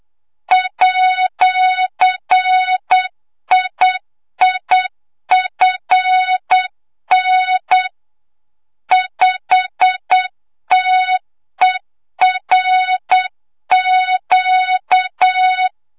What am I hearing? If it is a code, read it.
Morse code, heard as a keyed tone, about 8 wpm, 3700 Hz, with noise. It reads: @IIFN 5TERQ
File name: Morze-HA2MN / hangp / 2 06 GYAKORLAT